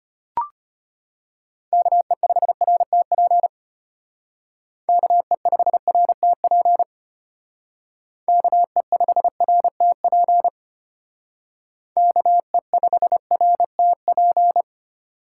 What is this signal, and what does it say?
Morse code, heard as a keyed tone, about 31 wpm, 700 Hz. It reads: KE5RTP KE5RTP KE5RTP KE5RTP
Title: Speed Racing - USA 2x3 Call Signs 25wpm